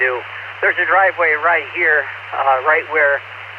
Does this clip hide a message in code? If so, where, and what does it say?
Yes: Emergency Alert System (SAME) tones from 0.4 to 3.3 s, heard as harsh buzzing bursts, with speech.